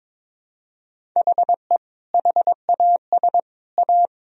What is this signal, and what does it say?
Morse code, heard as a keyed tone, about 22 wpm, 700 Hz. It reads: HE HAS A
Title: Sentences from Top 500 Words and Abbreviations 22wpm